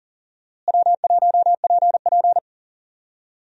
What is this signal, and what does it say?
Morse code, heard as a keyed tone, about 37 wpm, 700 Hz. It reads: W1PP
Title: USA Call Signs - 4 Characters - Repeated 2x - 40wpm